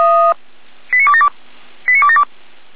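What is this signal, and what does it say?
Five-tone selective call (ZVEI), heard as quick runs of five tones, tones of about 70 ms. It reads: 89272 89272
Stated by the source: ZVEI selective call received on SDR.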